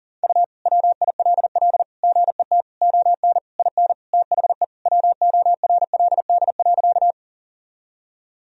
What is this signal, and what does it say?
Morse code, heard as a keyed tone, about 40 wpm, 700 Hz. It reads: U WILL GET ON IN THE WORLD.
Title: Sentences from Top 200 Words and Abbreviations 40wpm